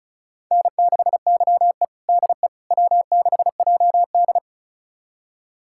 Morse code, heard as a keyed tone, about 36 wpm, 700 Hz. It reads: N6YE DE W6JD
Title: USA Callsign DE Callsign - 4 Characters - 35wpm